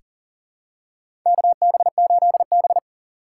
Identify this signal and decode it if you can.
Morse code, heard as a keyed tone, about 40 wpm, 700 Hz. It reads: KB8B
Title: USA Call Signs - 4 Characters - 4x Character Spacing - 40wpm